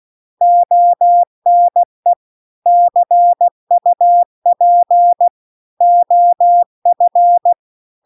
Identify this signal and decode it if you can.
Morse code, heard as a keyed tone, about 16 wpm, 700 Hz. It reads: ONE CUP OF
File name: CW-The-Big-Twoheaded-River-2-16WPM